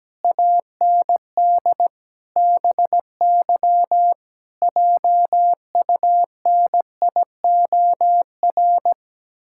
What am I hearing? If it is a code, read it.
Morse code, heard as a keyed tone, about 17 wpm, 700 Hz. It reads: AND BY JUNIOR
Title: Pangram Challenge v2 - 17wpm